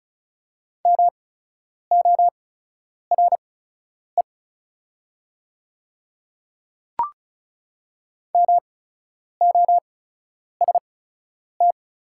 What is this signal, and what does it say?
Morse code, heard as a keyed tone, about 36 wpm, spaced as for 10 wpm, 700 Hz. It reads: MORE MOST
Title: 8x Character Spacing - Top 200 Words - 35wpm